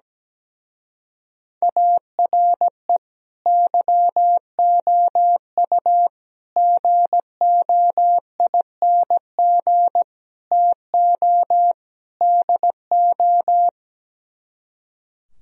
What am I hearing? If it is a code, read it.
Morse code, heard as a keyed tone, about 17 wpm, 700 Hz. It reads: ARE YOU GOING TO DO
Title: Speed-Racing - Sets of 5 Words 17wpm